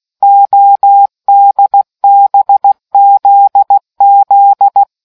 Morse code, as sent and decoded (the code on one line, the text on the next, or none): --- -.. -... --.. --..
ODBZZ